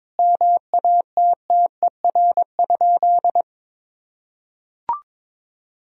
Morse code, heard as a keyed tone, about 22 wpm, 700 Hz. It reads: MATTER?
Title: Sentences from Top 1000 Words - Review Part 4 - 22wpm